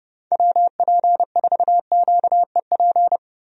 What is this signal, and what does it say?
Morse code, heard as a keyed tone, about 30 wpm, 700 Hz. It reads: WP4QEP
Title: USA Call Signs - 6 Characters - Repeated 3x - 30wpm